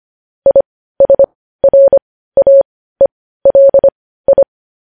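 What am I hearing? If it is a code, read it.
Morse code, heard as a keyed tone, about 25 wpm, spaced as for 15 wpm, 550 Hz. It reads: ISRAELI
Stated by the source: Morse code practice of news headlines at 15 wpm.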